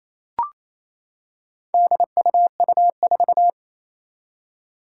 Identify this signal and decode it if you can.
Morse code, heard as a keyed tone, about 27 wpm, 700 Hz. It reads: DUU4
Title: Four Challenging Letter-Number Combinations 28wpm